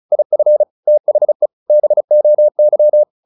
Morse code, heard as a keyed tone, about 35 wpm, 600 Hz. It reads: IF THE BOY
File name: the old man and the sea / the old man and the sea 1 35wpm